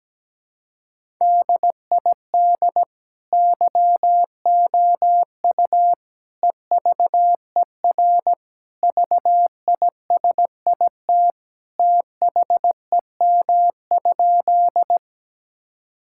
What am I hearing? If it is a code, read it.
Morse code, heard as a keyed tone, about 17 wpm, 700 Hz. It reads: DID YOU EVER VISIT THEM?
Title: Sentences from Top 1000 Words - Review Part 6 - 17wpm